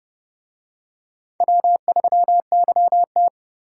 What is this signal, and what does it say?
Morse code, heard as a keyed tone, about 30 wpm, 700 Hz. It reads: W3YT